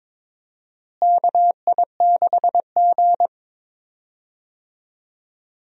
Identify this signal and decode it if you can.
Morse code, heard as a keyed tone, about 22 wpm, 700 Hz. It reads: KI6G
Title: USA Call Signs - 4 Characters - Repeated 3x - 22wpm